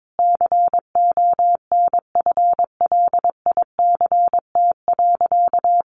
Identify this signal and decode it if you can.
Morse code, heard as a keyed tone, about 22 wpm, 700 Hz. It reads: CONFLICT.
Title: Sentences from Top 1300 Words 22wpm